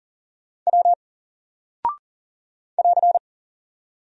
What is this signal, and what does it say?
Morse code, heard as a keyed tone, about 41 wpm, 700 Hz. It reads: W <AR>
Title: Single Character (Letter:Number:Punctuation:Prosign) 40wpm